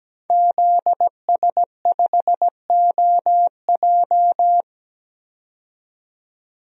Morse code, heard as a keyed tone, about 17 wpm, 700 Hz. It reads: ZS5OJ
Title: DXCC-100 Call Signs - 5 Characters - 17wpm